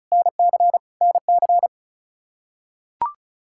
Morse code, heard as a keyed tone, about 36 wpm, 700 Hz. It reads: NC NC
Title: Practice QSO - Exchange 1 - Beginning - 35wpm